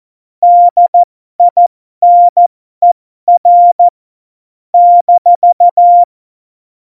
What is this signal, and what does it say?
Morse code, heard as a keyed tone, about 14 wpm, spaced as for 12 wpm, 700 Hz. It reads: DINER -
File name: Morsekurs / Witze